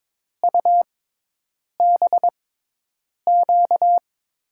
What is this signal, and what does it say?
Morse code, heard as a keyed tone, about 22 wpm, spaced as for 8 wpm, 700 Hz. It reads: UBQ
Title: USA Call Signs - 6x Character Spacing - 22wpm